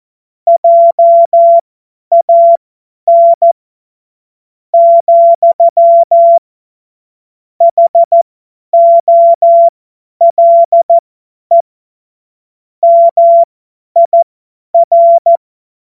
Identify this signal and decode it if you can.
Morse code, heard as a keyed tone, about 14 wpm, spaced as for 9 wpm, 680 Hz. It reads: JAN, HOLE MIR
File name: Morsekurs / Witze / Witz 16. Februar 2021, 50 F70